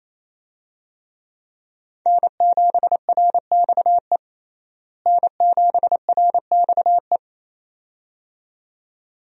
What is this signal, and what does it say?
Morse code, heard as a keyed tone, about 28 wpm, 700 Hz. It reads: N7RXE N7RXE
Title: USA Call Signs - 5 Characters - Repeated 2x - 28wpm